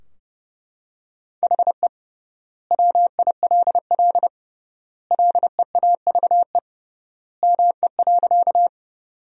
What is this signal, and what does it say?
Morse code, heard as a keyed tone, about 30 wpm, 700 Hz. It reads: HE WILL LEAVE ME.
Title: 3x Word-Spacing - Sentences from Top 400 Words - 30wpm